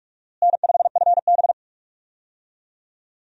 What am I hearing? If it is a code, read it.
Morse code, heard as a keyed tone, about 46 wpm, 700 Hz. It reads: N5FB
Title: USA Call Signs - 4 Characters - Repeated 2x - 45wpm